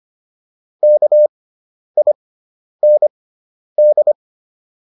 Morse code, heard as a keyed tone, about 25 wpm, spaced as for 10 wpm, 600 Hz. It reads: KIND